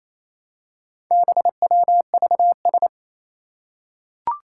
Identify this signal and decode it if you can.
Morse code, heard as a keyed tone, about 29 wpm, 700 Hz. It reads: BWVS
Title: Four Challenging Letter-Number Combinations 28wpm